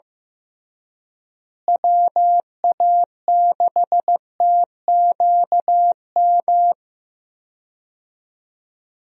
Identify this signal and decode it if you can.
Morse code, heard as a keyed tone, about 15 wpm, 700 Hz. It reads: WA6TQM
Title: USA Call Signs - 6 Characters - Repeated 2x - 15wpm